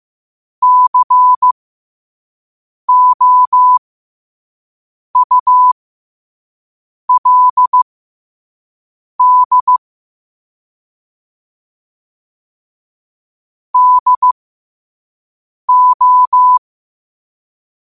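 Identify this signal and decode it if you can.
Morse code, heard as a keyed tone, about 15 wpm, spaced as for 5 wpm, 1000 Hz. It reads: COULD DO
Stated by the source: Morse code sent at 5 words per minute.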